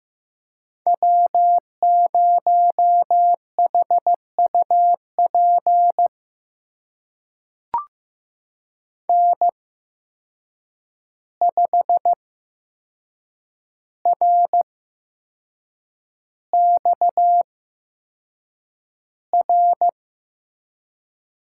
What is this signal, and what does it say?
Morse code, heard as a keyed tone, about 15 wpm, spaced as for 6 wpm, 700 Hz. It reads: W0HUP N5RXR
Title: USA Call Signs - 5 Characters - 8x Character Spacing - 15wpm